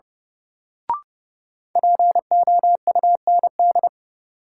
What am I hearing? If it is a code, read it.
Morse code, heard as a keyed tone, about 32 wpm, 700 Hz. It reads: POUND
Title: Sentences from Top 1000 Words - Review Part 2 - New Word Drill - 30wpm